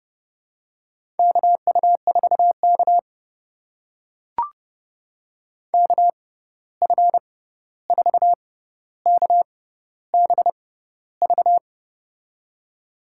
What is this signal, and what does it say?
Morse code, heard as a keyed tone, about 30 wpm, spaced as for 14 wpm, 700 Hz. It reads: KU4K KF4KBV
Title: USA Call Signs - 6x Character Spacing - 30wpm